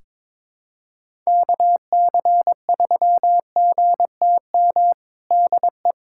Morse code, heard as a keyed tone, about 22 wpm, 700 Hz. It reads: KC3GTM DE
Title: USA Callsign DE Callsign - 6 Characters - 22wpm